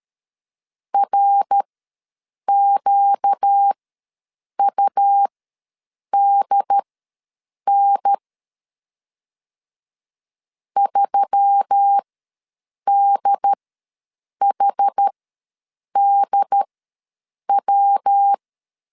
Morse code, heard as a keyed tone, about 13 wpm, spaced as for 7 wpm, 800 Hz. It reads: RQUDN 3DHDW
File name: LEZIONI / 5 LBVC / 5 L ws 12wpm 2958